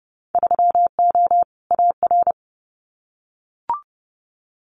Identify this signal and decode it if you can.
Morse code, heard as a keyed tone, about 30 wpm, 700 Hz. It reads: 3O AR
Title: Winter Field Day Exchanges Encore 30wpm